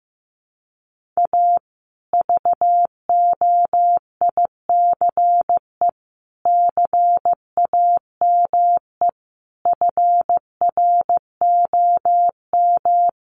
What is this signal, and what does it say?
Morse code, heard as a keyed tone, about 15 wpm, 700 Hz. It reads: A VOICE CAME FROM
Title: Sentences from Top 1400 Words 15wpm